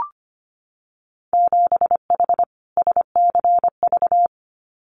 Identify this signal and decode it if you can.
Morse code, heard as a keyed tone, about 25 wpm, 700 Hz. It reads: 7H SCV
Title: Winter Field Day Exchanges Encore 25wpm